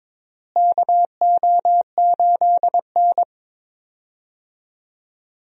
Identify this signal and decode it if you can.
Morse code, heard as a keyed tone, about 22 wpm, 700 Hz. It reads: KO8N
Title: USA Call Signs - 4 Characters - Repeated 3x - 22wpm